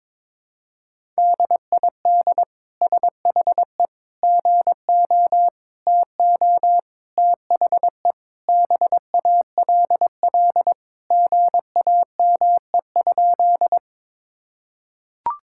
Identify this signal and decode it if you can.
Morse code, heard as a keyed tone, about 22 wpm, 700 Hz. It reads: DID SHE GO TO THE BALL GAME?
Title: Sentences from Top 1000 Words - Review Part 6 - 22wpm